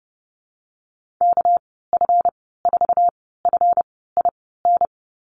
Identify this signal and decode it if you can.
Morse code, heard as a keyed tone, about 30 wpm, spaced as for 17 wpm, 700 Hz. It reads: KF4FIN
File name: Lesson 37g - Letter x - Call Signs 30wpm